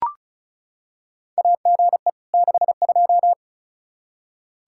Morse code, heard as a keyed tone, about 34 wpm, 700 Hz. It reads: AGE 62
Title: Practice QSO - Exchange 3 - More Info - 35wpm